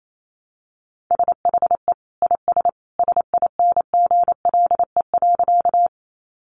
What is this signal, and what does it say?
Morse code, heard as a keyed tone, about 28 wpm, 700 Hz. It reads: SHE IS SINGLE.
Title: Sentences from Top 700 Words 28wpm